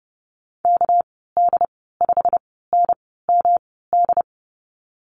Morse code, heard as a keyed tone, about 30 wpm, spaced as for 17 wpm, 700 Hz. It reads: KD5NMD
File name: Lesson 37g - Letter x - Call Signs 30wpm